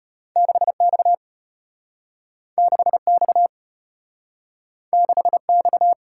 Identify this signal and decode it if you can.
Morse code, heard as a keyed tone, about 33 wpm, 700 Hz. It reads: 6X 6X 6X